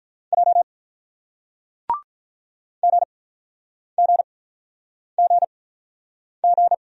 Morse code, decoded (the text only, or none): J GGGG